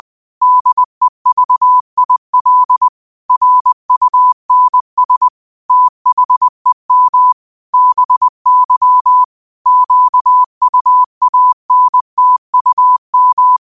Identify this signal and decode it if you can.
Morse code, heard as a keyed tone, about 20 wpm, 1000 Hz. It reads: DEVIL RUNS THEM BY QUANTUM
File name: qotdfiles / 20wpm